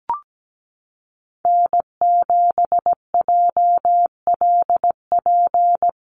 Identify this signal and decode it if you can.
Morse code, heard as a keyed tone, about 17 wpm, 700 Hz. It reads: N7JLP
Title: USA Call Signs Repeated 2x 17wpm